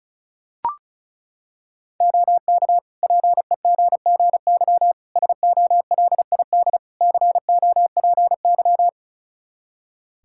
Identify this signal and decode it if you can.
Morse code, heard as a keyed tone, about 36 wpm, 700 Hz. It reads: OK PEGGY SOLID COPY
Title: Practice QSO - Exchange 3 - More Info - 35wpm